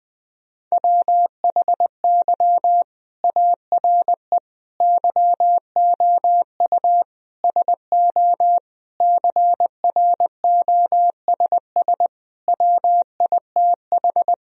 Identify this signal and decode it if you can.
Morse code, heard as a keyed tone, about 21 wpm, 700 Hz. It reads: WHY ARE YOU SO CROSS WITH
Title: Sentences from Top 1000 Words - Review Part 1 - 20wpm